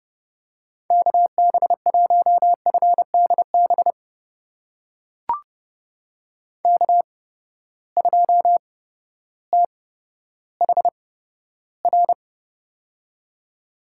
Morse code, heard as a keyed tone, about 30 wpm, spaced as for 14 wpm, 700 Hz. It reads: KB1FDB K2THR
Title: USA Call Signs - 8x Character Spacing - 30wpm